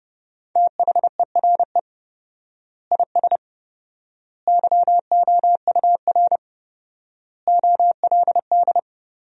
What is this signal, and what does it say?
Morse code, heard as a keyed tone, about 32 wpm, 700 Hz. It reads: THERE IS YOUR OLD
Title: 4x Word-Spacing - Sentences from Top 500 Words - 30wpm